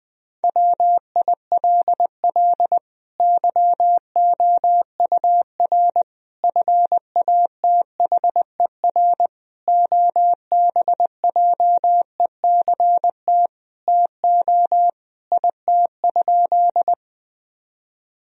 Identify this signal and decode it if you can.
Morse code, heard as a keyed tone, about 20 wpm, 700 Hz. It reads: WILL YOUR FATHER OBJECT TO IT?
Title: Sentences from Top 1000 Words - Review Part 3 - 20wpm